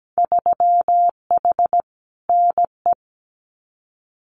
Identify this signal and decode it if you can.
Morse code, heard as a keyed tone, about 17 wpm, 700 Hz. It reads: 3H NE
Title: Winter Field Day Exchanges Encore 17wpm